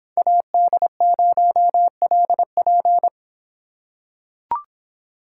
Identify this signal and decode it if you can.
Morse code, heard as a keyed tone, about 24 wpm, 700 Hz. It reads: AD0LP